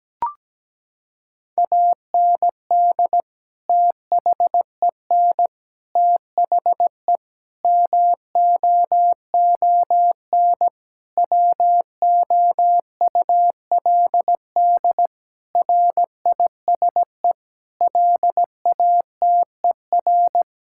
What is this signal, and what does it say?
Morse code, heard as a keyed tone, about 17 wpm, 700 Hz. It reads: AND THEN THE MOON WOULD RISE LATER
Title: Sentences from Top 1000 Words - Review Part 6 - 17wpm